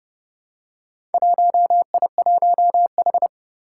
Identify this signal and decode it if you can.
Morse code, heard as a keyed tone, about 30 wpm, 700 Hz. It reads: 1I1H
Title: Four Challenging Letter-Number Combinations 30wpm